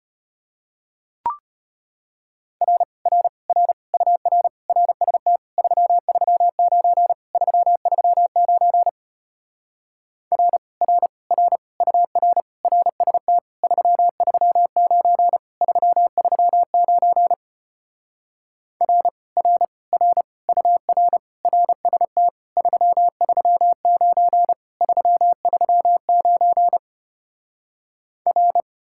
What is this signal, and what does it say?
Morse code, heard as a keyed tone, about 33 wpm, 700 Hz. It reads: R R R UR RST 339 339 R R R UR RST 339 339 R R R UR RST 339 339 R